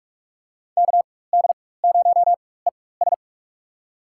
Morse code, heard as a keyed tone, about 42 wpm, spaced as for 21 wpm, 700 Hz. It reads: KD0ES
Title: USA Call Signs - 4x Character Spacing - 45wpm